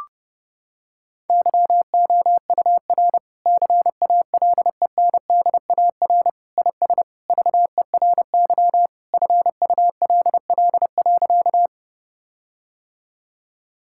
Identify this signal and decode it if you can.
Morse code, heard as a keyed tone, about 30 wpm, 700 Hz. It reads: YOUR CALENDAR IS VERY FULL.
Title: Sentences from Top 1600 Words 30wpm